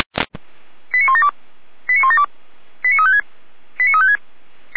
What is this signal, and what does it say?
Five-tone selective call (ZVEI), heard as quick runs of five tones, tones of about 80 ms. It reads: 89172 89172 89356 89356
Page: ZVEI